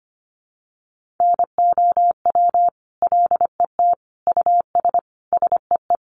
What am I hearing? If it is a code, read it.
Morse code, heard as a keyed tone, about 25 wpm, 700 Hz. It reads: NOW LET US SEE
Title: Sentences from Top 1400 Words 25wpm